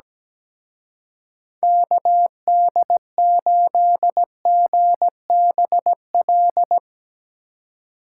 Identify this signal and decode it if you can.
Morse code, heard as a keyed tone, about 19 wpm, 700 Hz. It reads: KD8GBL
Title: Speed Racing - USA 2x3 Call Signs 17wpm